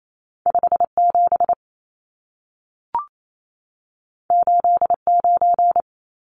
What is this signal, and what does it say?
Morse code, heard as a keyed tone, about 28 wpm, 700 Hz. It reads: 57 89
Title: Numbers - 2-Digit 28wpm